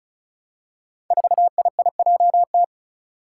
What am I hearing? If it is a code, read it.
Morse code, heard as a keyed tone, about 36 wpm, 700 Hz. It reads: 4IIJT
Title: Five Challenging Letter-Number Combinations 35wpm